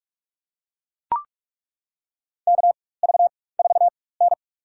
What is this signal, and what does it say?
Morse code, heard as a keyed tone, about 42 wpm, spaced as for 20 wpm, 700 Hz. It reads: KV4N
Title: USA Call Signs - 4x Character Spacing - 45wpm